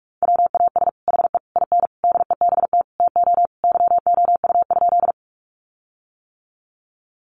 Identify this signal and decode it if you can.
Morse code, heard as a keyed tone, about 44 wpm, 700 Hz. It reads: WAS HE IN DEBT TO YOU?
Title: Sentences from Top 1300 Words 45wpm